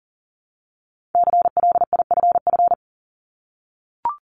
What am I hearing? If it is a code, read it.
Morse code, heard as a keyed tone, about 40 wpm, 690 Hz. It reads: CLIFF
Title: Multi-letter Phonemes 40wpm